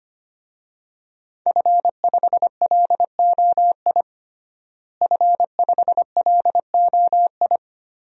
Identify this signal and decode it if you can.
Morse code, heard as a keyed tone, about 25 wpm, 700 Hz. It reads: F5LOI F5LOI
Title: DXCC-100 Call Signs - 5 Characters - Repeated 3x - 25wpm